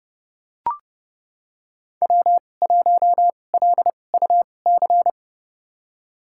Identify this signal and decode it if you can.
Morse code, heard as a keyed tone, about 29 wpm, spaced as for 21 wpm, 700 Hz. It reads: W1LUC
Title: USA Call Signs - 2x Character Spacing - 30wpm